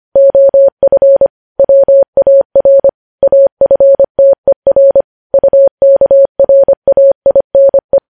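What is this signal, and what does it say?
Morse code, heard as a keyed tone, about 25 wpm, 550 Hz. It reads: OF WAR AFTER UKRAINE
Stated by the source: Morse code news headlines sent at 25 wpm.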